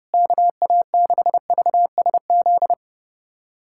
Morse code, heard as a keyed tone, about 30 wpm, 700 Hz. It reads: KA6VSZ
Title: USA Callsign DE Callsign - 6 Characters - 30wpm